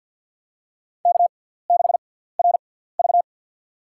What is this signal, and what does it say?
Morse code, heard as a keyed tone, about 54 wpm, spaced as for 17 wpm, 690 Hz. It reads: K6RV
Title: USA Call Signs - 4 Characters - 6x Character Spacing - 50wpm